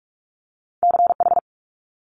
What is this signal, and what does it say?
Morse code, heard as a keyed tone, about 46 wpm, 700 Hz. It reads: CH
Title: Multi-letter Phonemes Quick Fire 45wpm